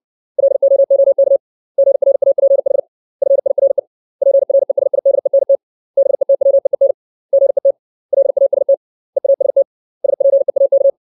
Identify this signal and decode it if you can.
Morse code, heard as a keyed tone, about 60 wpm, 550 Hz. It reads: LOOK GRAPH FIRE PRESIDENT BETWEEN GET LAST EAST SPEAK
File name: 632 words / 632 Words 60WPM